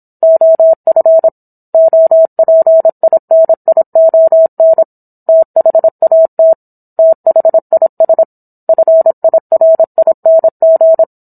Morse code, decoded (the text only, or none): OF OPINION THAT THIS FIRING